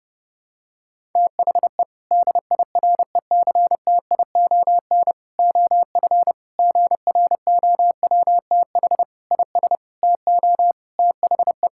Morse code, heard as a keyed tone, about 30 wpm, 700 Hz. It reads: THE DIRECTION OF GROWTH IS TO THE
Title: Sentences from Top 1000 Words - Review Part 5 - 30wpm